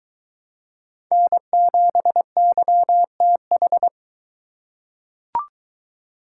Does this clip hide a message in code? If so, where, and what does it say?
Yes: Morse code from 1.1 to 3.9 s, heard as a keyed tone, about 21 wpm, 700 Hz. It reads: N7YTH